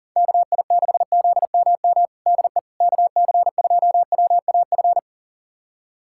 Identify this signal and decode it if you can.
Morse code, heard as a keyed tone, about 40 wpm, 700 Hz. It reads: KI6ZMM DE KC2WAF
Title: USA Callsign DE Callsign 40wpm